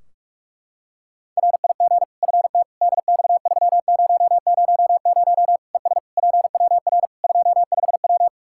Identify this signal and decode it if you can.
Morse code, heard as a keyed tone, about 46 wpm, 700 Hz. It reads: RIG FT DX3000 ES PWR 25W
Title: K1USN SST QSO Practice 45wpm